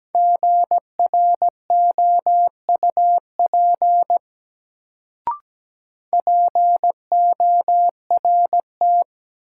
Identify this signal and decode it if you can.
Morse code, heard as a keyed tone, about 17 wpm, 700 Hz. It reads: GROUP PORT
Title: Sentences from Top 1000 Words - Review Part 1 - New Word Drill - 17wpm